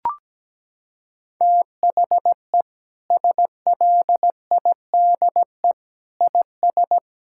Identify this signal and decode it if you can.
Morse code, heard as a keyed tone, about 17 wpm, 700 Hz. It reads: THE SLIDE IS
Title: Sentences from Top 1600 Words 17wpm